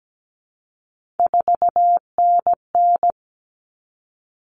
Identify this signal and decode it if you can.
Morse code, heard as a keyed tone, about 17 wpm, 700 Hz. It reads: 4NN
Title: Common Cut Numbers - 3-Digit 17wpm